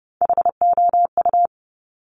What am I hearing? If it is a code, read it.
Morse code, heard as a keyed tone, about 30 wpm, 690 Hz. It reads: HOU
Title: Top Three Letter Combinations in Words 30wpm